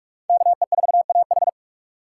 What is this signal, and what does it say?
Morse code, heard as a keyed tone, about 58 wpm, 690 Hz. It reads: KE4AH